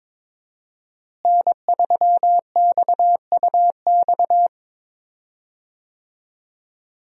Morse code, heard as a keyed tone, about 22 wpm, 700 Hz. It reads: N3XUX
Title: USA Call Signs - 5 Characters - Repeated 2x - 22wpm